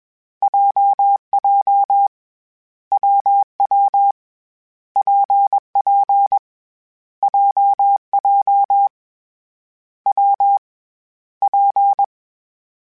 Morse code, heard as a keyed tone, about 21 wpm, 800 Hz. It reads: JJ WW PP JJ W P
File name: Difficult Combos / Difficult Combos-mp3s / WPJ WPJ1 @20